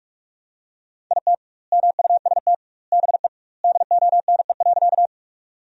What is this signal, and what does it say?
Morse code, heard as a keyed tone, about 43 wpm, 700 Hz. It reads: IT MUST BE DONE.
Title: 2x Word-Spacing - Sentences from Top 500 Words - 45wpm